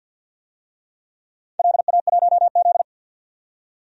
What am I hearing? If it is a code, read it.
Morse code, heard as a keyed tone, about 52 wpm, 700 Hz. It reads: LA1Z
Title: DXCC-100 Call Signs - 4 Characters - 50wpm